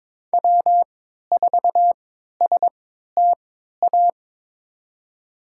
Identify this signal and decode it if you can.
Morse code, heard as a keyed tone, about 22 wpm, spaced as for 13 wpm, 700 Hz. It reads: W4STA
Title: USA Call Signs - 3x Character Spacing - 22wpm